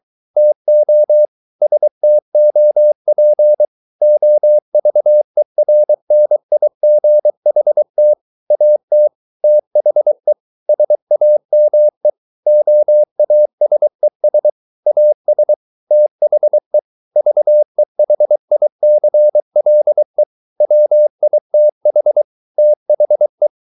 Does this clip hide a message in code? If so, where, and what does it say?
Yes: Morse code from 0.4 to 23.5 s, heard as a keyed tone, about 23 wpm, 600 Hz. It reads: TO STOP OVERNIGHT AT THE SAME OASES AS THE VEHICLE WITH THE